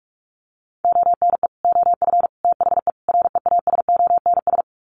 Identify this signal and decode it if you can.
Morse code, heard as a keyed tone, about 44 wpm, 700 Hz. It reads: ONE OF THE REASONS
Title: Sets of 4 Words 45wpm